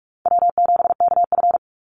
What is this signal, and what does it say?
Morse code, heard as a keyed tone, about 44 wpm, 700 Hz. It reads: W7KF
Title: Call Signs Repeated 3x - 45wpm